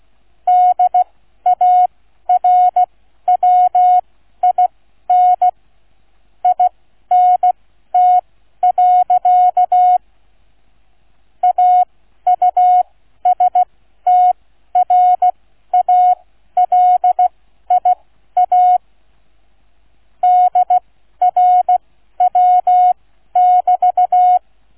Morse code, heard as a keyed tone, about 16 wpm, spaced as for 12 wpm, 730 Hz. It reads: DARWIN INT. AUSTRALIA DRW<BT>